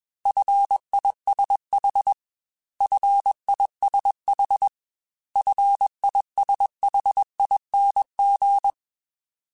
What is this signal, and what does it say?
Morse code, heard as a keyed tone, about 21 wpm, 800 Hz. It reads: FISH FISH FISHING
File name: F and L / F and L mp3 / F and L Prefixes 3x2 Twice @20